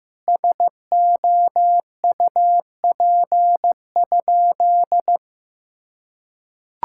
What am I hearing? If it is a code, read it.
Morse code, heard as a keyed tone, about 16 wpm, 700 Hz. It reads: SOUP?
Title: Sentences from Top 1000 Words - Review Part 6 - 15wpm